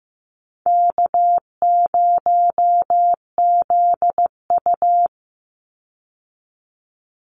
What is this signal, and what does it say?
Morse code, heard as a keyed tone, about 15 wpm, 700 Hz. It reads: K0ZU
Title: Easy Call Signs - 15wpm